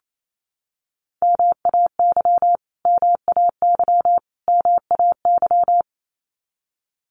Morse code, heard as a keyed tone, about 28 wpm, 700 Hz. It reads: MAY MAY MAY
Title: Top 200 Words Repeated 28wpm